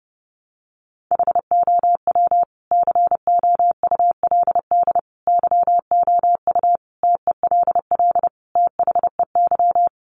Morse code, heard as a keyed tone, about 30 wpm, 700 Hz. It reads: HOW COULD YOU TELL THEY
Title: Sentences from Top 1000 Words 30wpm